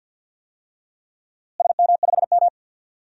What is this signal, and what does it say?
Morse code, heard as a keyed tone, about 53 wpm, 700 Hz. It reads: SM5M
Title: DXCC-100 Call Signs - 4 Characters - Repeated 3x - 50wpm